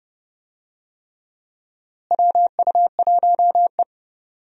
Morse code, heard as a keyed tone, about 30 wpm, 700 Hz. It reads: WU1E
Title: USA Call Signs - 4 Characters - Repeated 3x - 30wpm